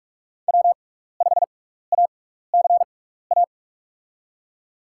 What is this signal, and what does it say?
Morse code, heard as a keyed tone, about 46 wpm, spaced as for 15 wpm, 700 Hz. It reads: W5ACA
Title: USA Call Signs - 5 Characters - 6x Character Spacing - 45wpm